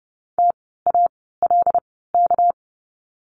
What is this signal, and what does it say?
Morse code, heard as a keyed tone, about 30 wpm, spaced as for 17 wpm, 700 Hz. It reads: TALK
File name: course / 30:10wpm / 230 Lesson 37f - Letter x - Words 30wpm